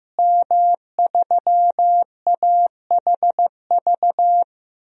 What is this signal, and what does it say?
Morse code, heard as a keyed tone, about 15 wpm, 700 Hz. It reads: M3AHV
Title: DXCC-100 Call Signs - 5 Characters - Repeated 2x - 15wpm